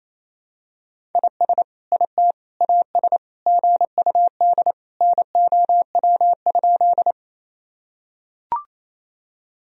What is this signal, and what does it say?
Morse code, heard as a keyed tone, about 28 wpm, 700 Hz. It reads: IS IT AS GUD NOW?
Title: Sentences from Top 200 Words and Abbreviations 28wpm